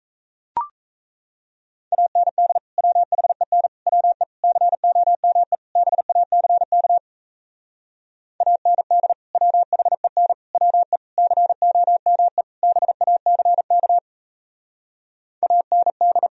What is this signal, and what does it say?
Morse code, heard as a keyed tone, about 37 wpm, 700 Hz. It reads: AND WHEN WE COME BACK AND WHEN WE COME BACK AND